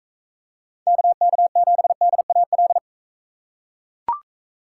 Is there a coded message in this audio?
Yes: Morse code from 0.9 to 2.8 s, heard as a keyed tone, about 39 wpm, 700 Hz. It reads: KK7DAL